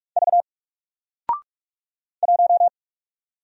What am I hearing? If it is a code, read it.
Morse code, heard as a keyed tone, about 46 wpm, 700 Hz. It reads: V 1
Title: Single Letter-Number Alt 45wpm